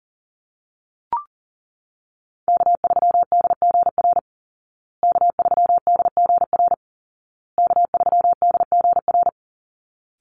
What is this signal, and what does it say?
Morse code, heard as a keyed tone, about 40 wpm, 700 Hz. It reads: K3DGR K3DGR K3DGR
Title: Call Signs Repeated 2x - 40wpm